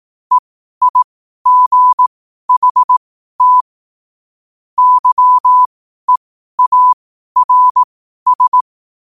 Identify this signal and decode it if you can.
Morse code, heard as a keyed tone, about 18 wpm, spaced as for 12 wpm, 1000 Hz. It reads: EIGHT YEARS
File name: qotdfiles / 12wpm